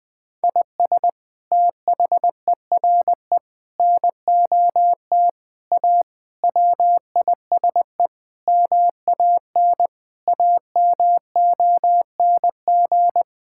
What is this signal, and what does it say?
Morse code, heard as a keyed tone, about 21 wpm, 700 Hz. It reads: IS THERE NOT A WISE MAN AMONG
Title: Sentences from Top 1600 Words 20wpm